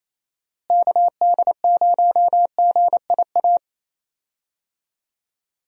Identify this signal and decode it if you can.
Morse code, heard as a keyed tone, about 28 wpm, 700 Hz. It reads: KD0GIA
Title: USA Call Signs - 6 Characters - Repeated 3x - 28wpm